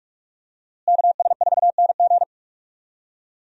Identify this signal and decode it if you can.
Morse code, heard as a keyed tone, about 46 wpm, 700 Hz. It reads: KS4NG
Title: USA Call Signs - 5 Characters - Repeated 2x - 45wpm